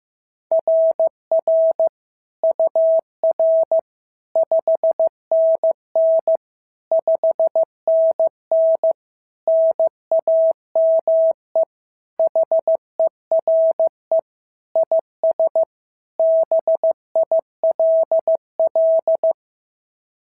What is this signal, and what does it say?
Morse code, heard as a keyed tone, about 14 wpm, 650 Hz. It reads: RR UR 5NN 5NN NAME HERE IS BILL
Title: SOTA QSO Practice 15wpm